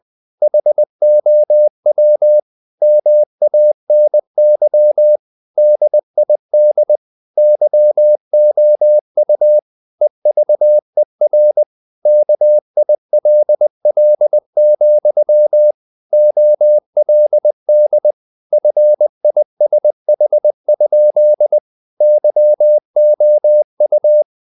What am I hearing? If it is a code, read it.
Morse code, heard as a keyed tone, about 20 wpm, 600 Hz. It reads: HOW MANY DID YOU EVER KILL, OLD FISH? YOU
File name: the old man and the sea part 2 20wpm